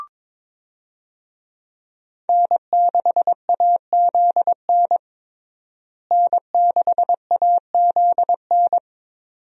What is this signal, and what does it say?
Morse code, heard as a keyed tone, about 22 wpm, 700 Hz. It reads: N6AZN N6AZN
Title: USA Call Signs - 5 Characters - Repeated 2x - 22wpm